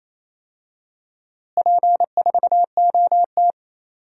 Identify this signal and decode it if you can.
Morse code, heard as a keyed tone, about 28 wpm, 700 Hz. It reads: P4OT
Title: DXCC-100 Call Signs - 4 Characters - Repeated 2x - 28wpm